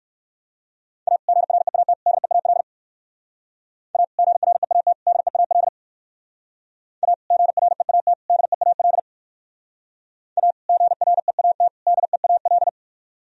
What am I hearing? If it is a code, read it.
Morse code, heard as a keyed tone, about 58 wpm, 700 Hz. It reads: A GREAT DEAL A GREAT DEAL A GREAT DEAL A GREAT DEAL